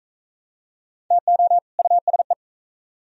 Morse code, heard as a keyed tone, about 44 wpm, 700 Hz. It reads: TO USE